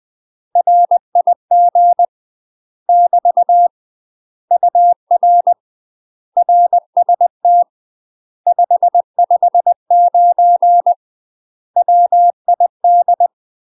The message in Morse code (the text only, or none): RIG <BT> UR RST 559 WID